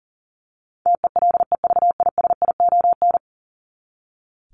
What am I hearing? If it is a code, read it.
Morse code, heard as a keyed tone, about 40 wpm, 700 Hz. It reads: TELEVISION
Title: Top 501-750 Words 40wpm